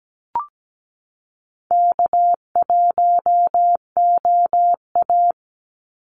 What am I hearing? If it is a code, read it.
Morse code, heard as a keyed tone, about 17 wpm, 700 Hz. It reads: K1OA
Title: Call Signs Repeated 2x - 17wpm